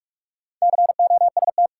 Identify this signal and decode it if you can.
Morse code, heard as a keyed tone, about 45 wpm, 700 Hz. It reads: COST
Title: Sets of 3 Words - New Word Drill - 45wpm